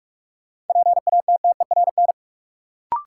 Morse code, heard as a keyed tone, about 42 wpm, 700 Hz. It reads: PATTERN
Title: Sentences from Top 1000 Words - New Word Drill - 45wpm